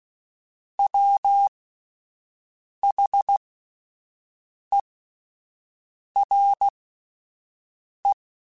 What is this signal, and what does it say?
Morse code, heard as a keyed tone, about 16 wpm, spaced as for 5 wpm, 800 Hz. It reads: WHERE